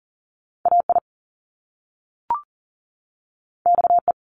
Morse code, heard as a keyed tone, about 42 wpm, 700 Hz. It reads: AI XE
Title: Top Two Letter Combinations in Words 40wpm